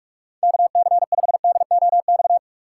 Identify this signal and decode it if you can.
Morse code, heard as a keyed tone, about 45 wpm, 700 Hz. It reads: KC5DOX
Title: USA Call Signs - 2x Character Spacing - 45wpm